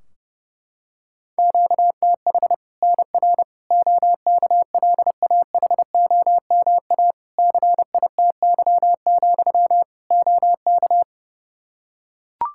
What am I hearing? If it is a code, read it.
Morse code, heard as a keyed tone, about 32 wpm, 700 Hz. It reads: QTH NR OKLAHOMA CITY, OK
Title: Practice QSO - Exchange 1 - Beginning - 30wpm